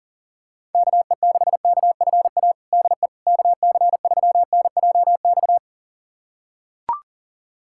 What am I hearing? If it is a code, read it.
Morse code, heard as a keyed tone, about 39 wpm, 700 Hz. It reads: KE6KFA DE KC3NJX
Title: USA Callsign DE Callsign - 6 Characters - 40wpm